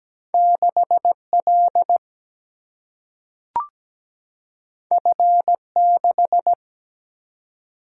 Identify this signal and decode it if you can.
Morse code, heard as a keyed tone, about 17 wpm, 700 Hz. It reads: 6 L F 6
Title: Single-Letter Single-Number 17wpm